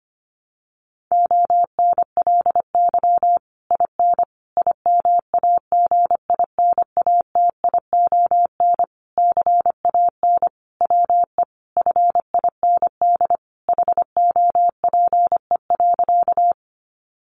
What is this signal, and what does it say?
Morse code, heard as a keyed tone, about 25 wpm, 700 Hz. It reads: ONLY IN IMAGINATION CAN WE FIND HOPE.
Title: Sentences from Top 1300 Words 25wpm